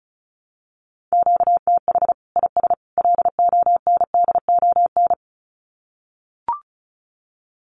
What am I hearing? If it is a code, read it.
Morse code, heard as a keyed tone, about 34 wpm, 700 Hz. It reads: QTH IS LONDON
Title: Common QSO Phrases 35wpm